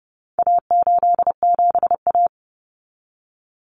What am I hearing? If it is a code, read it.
Morse code, heard as a keyed tone, about 30 wpm, 700 Hz. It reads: A87A
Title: Common Cut Numbers - 4-Digit 30wpm